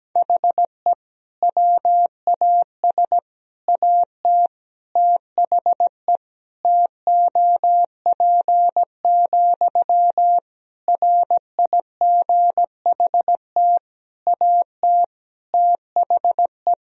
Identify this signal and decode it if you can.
Morse code, heard as a keyed tone, about 17 wpm, 700 Hz. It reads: HE WAS AT THE TOP, RIGHT AT THE
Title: Sentences from Top 1000 Words - Review Part 2 - 17wpm